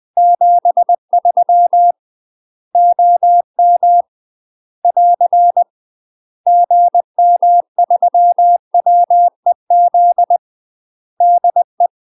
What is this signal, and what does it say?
Morse code, heard as a keyed tone, about 20 wpm, 700 Hz. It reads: 73 OM <AR> GM3WEZ DE
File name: UK MASTER QSOs TRAINING / 20WPM / UK MASTER QSOs 0007